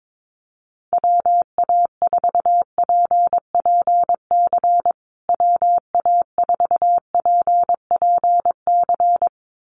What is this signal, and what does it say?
Morse code, heard as a keyed tone, about 22 wpm, 700 Hz. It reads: WA4PPC WA4PPC
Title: USA Call Signs Repeated 2x 22wpm